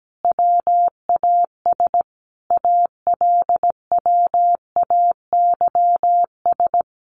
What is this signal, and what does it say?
Morse code, heard as a keyed tone, about 17 wpm, 700 Hz. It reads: WAS ALWAYS
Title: Sentences from Top 900 Words 17wpm